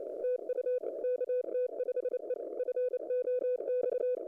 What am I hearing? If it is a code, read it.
Morse code, heard as a keyed tone, about 29 wpm, 500 Hz. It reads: TU KT5E FOX
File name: kt5e fox hunt audio 3 minutes 11-27-08